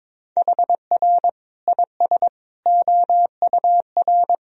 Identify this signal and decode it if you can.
Morse code, heard as a keyed tone, about 22 wpm, 700 Hz. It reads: HR IS OUR
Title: Sentences from Top 500 Words and Abbreviations 22wpm